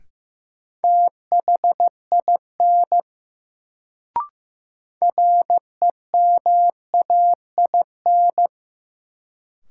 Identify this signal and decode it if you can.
Morse code, heard as a keyed tone, about 15 wpm, 700 Hz. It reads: THIN REMAIN
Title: Sentences from Top 1200 Words - New Word Drill - 15wpm